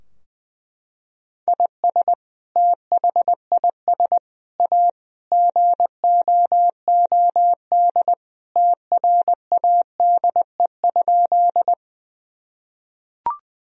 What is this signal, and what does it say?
Morse code, heard as a keyed tone, about 20 wpm, 700 Hz. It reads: IS THIS A GOOD TRADE?
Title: Sentences from Top 1000 Words - Review Part 4 - 20wpm